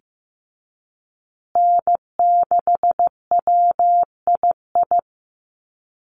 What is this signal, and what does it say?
Morse code, heard as a keyed tone, about 15 wpm, 700 Hz. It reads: N6WII
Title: USA Call Signs Repeated 3x 15wpm